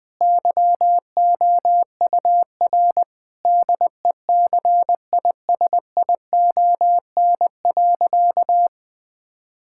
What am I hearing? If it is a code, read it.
Morse code, heard as a keyed tone, about 20 wpm, 700 Hz. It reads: YOUR DECISION.
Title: Sentences with 8-Word Phrases - 20wpm